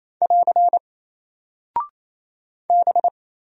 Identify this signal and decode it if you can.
Morse code, heard as a keyed tone, about 28 wpm, 700 Hz. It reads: <AR> B
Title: Single Character (Letter:Number:Punctuation:Prosign) Alt 28wpm